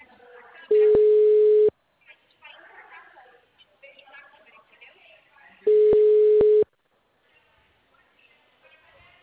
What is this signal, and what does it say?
Ringback tone (Europe, 425 Hz), with speech.